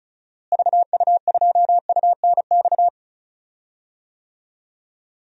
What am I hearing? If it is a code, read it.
Morse code, heard as a keyed tone, about 36 wpm, 700 Hz. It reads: VU2UNX
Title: DXCC-100 Call Signs - 6 Characters - Repeated 3x - 35wpm